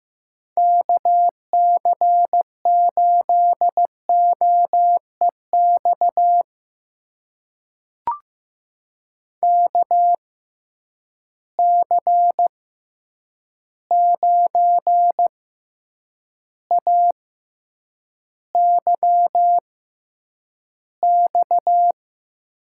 Morse code, heard as a keyed tone, about 15 wpm, spaced as for 8 wpm, 700 Hz. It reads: KC8OEX KC9AYX
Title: USA Call Signs - 6 Characters - 6x Character Spacing - 15wpm